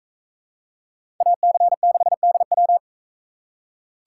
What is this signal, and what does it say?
Morse code, heard as a keyed tone, about 41 wpm, 700 Hz. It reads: AC6DW